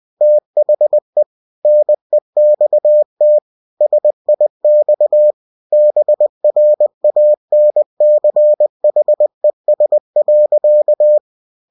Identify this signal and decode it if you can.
Morse code, heard as a keyed tone, about 20 wpm, 600 Hz. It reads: THE NEXT SIX BRANCHES.
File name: cw-books / winnie-the-pooh / winnie-20wpm-page-0004